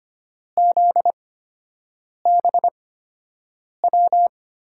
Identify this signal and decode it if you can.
Morse code, heard as a keyed tone, about 25 wpm, spaced as for 7 wpm, 700 Hz. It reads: ZBW
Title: USA Call Signs - 6 Characters - 8x Character Spacing - 25wpm